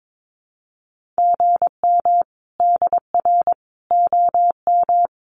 Morse code, heard as a keyed tone, about 22 wpm, 700 Hz. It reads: GM DR OM
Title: Common QSO Phrases 22wpm